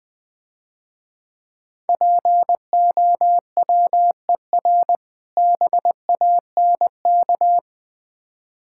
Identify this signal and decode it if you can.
Morse code, heard as a keyed tone, about 20 wpm, 700 Hz. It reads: POWER BANK
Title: Difficult Words - Speed-Racing with Hints - 20wpm